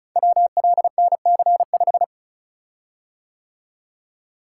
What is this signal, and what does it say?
Morse code, heard as a keyed tone, about 34 wpm, 700 Hz. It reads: WLNC5
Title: Code Groups - 5 Letter:Numbers - 35wpm